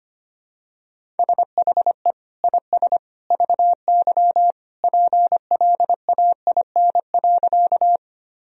Morse code, heard as a keyed tone, about 25 wpm, 700 Hz. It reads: SHE IS VY PLAIN.
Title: Sentences from Top 400 Words and Abbreviations 25wpm